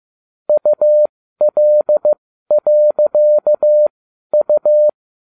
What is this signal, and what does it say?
Morse code, heard as a keyed tone, about 15 wpm, spaced as for 12 wpm, 600 Hz. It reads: UL.U